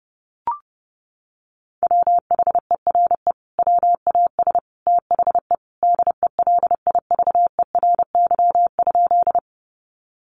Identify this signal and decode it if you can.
Morse code, heard as a keyed tone, about 30 wpm, 700 Hz. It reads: WHERE WAS THE DELIVERY?
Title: Sentences from Top 1500 Words 30wpm